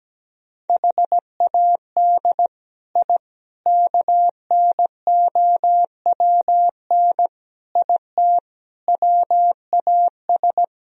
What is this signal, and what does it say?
Morse code, heard as a keyed tone, about 17 wpm, 700 Hz. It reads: HAD I KNOWN IT WAS
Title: Sentences from Top 1000 Words - Review Part 3 - 17wpm